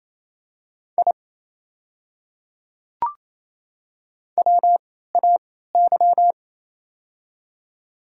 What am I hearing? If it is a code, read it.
Morse code, heard as a keyed tone, about 28 wpm, spaced as for 16 wpm, 700 Hz. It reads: I WAY
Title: 3x Character Spacing - Top 100 Words - 28wpm